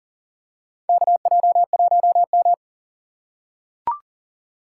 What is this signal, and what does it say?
Morse code, heard as a keyed tone, about 40 wpm, 700 Hz. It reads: KJ1M
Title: USA Call Signs - 4 Characters - 3x Character Spacing - 40wpm